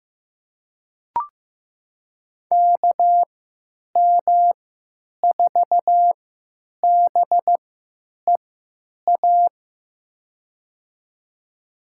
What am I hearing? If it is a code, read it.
Morse code, heard as a keyed tone, about 15 wpm, spaced as for 9 wpm, 700 Hz. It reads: KM4BEA
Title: USA Call Signs - 3x Character Spacing - 15wpm